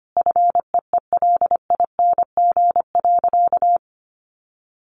Morse code, heard as a keyed tone, about 25 wpm, 700 Hz. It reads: FEELING.
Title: Sentences from Top 1400 Words 25wpm